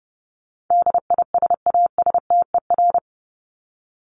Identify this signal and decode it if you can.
Morse code, heard as a keyed tone, about 30 wpm, 700 Hz. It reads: DISASTER
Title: Top 1501-1750 Words 30wpm